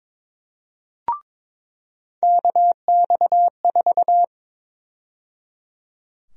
Morse code, heard as a keyed tone, about 22 wpm, 700 Hz. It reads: KX4
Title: USA Call Sign Prefixes - 22wpm